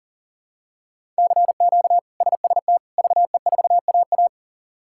Morse code, heard as a keyed tone, about 37 wpm, 700 Hz. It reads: CQ SST VE4AA
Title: K1USN SST QSO Practice - 40wpm